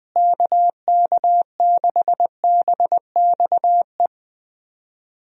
Morse code, heard as a keyed tone, about 21 wpm, 700 Hz. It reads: KK6BXE
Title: USA Call Signs - 6 Characters - Repeated 3x - 20wpm